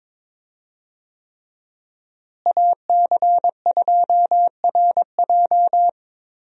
Morse code, heard as a keyed tone, about 22 wpm, 700 Hz. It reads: AC2RJ
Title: USA Call Signs - 5 Characters - Repeated 3x - 22wpm